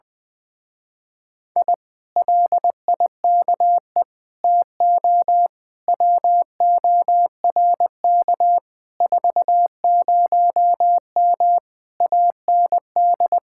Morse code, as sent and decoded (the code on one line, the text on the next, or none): .. / .-.. .. -.- . / - --- / .-- --- .-. -.- / ....- ----- -- / .- -. -..
I LIKE TO WORK 40M AND